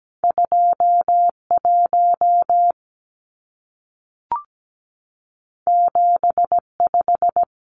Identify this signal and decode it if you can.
Morse code, heard as a keyed tone, about 17 wpm, 700 Hz. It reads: 21 75
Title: Numbers - 2-Digit 17wpm